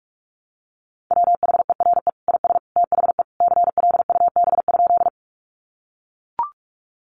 Morse code, heard as a keyed tone, about 44 wpm, 700 Hz. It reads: WHERE IS THE CLUB?
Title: Sentences from Top 1200 Words 45wpm